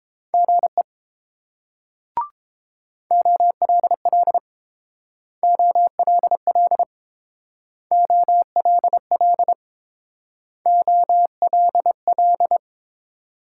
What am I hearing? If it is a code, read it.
Morse code, heard as a keyed tone, about 30 wpm, 700 Hz. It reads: GE OLL OLL OLL OLL